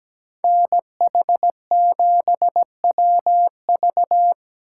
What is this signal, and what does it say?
Morse code, heard as a keyed tone, about 20 wpm, 700 Hz. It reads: NH7WV
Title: Speed Racing - USA 2x2 Call Signs 17wpm